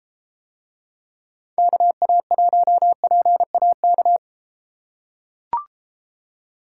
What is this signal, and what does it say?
Morse code, heard as a keyed tone, about 33 wpm, 700 Hz. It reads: KA1PAK